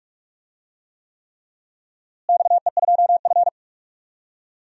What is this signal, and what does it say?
Morse code, heard as a keyed tone, about 46 wpm, 700 Hz. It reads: XE2F
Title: DXCC-100 Call Signs - 4 Characters - Repeated 3x - 45wpm